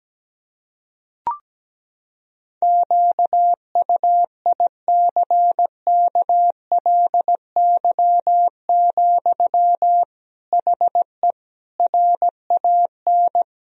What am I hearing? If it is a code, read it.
Morse code, heard as a keyed tone, about 17 wpm, 700 Hz. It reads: QUICKLY, HE RAN
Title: Sentences from Top 1000 Words - Review Part 3 - 17wpm